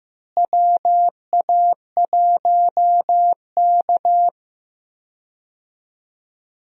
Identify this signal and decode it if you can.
Morse code, heard as a keyed tone, about 15 wpm, 700 Hz. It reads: WA1K
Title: USA Call Signs - 4 Characters - Repeated 3x - 15wpm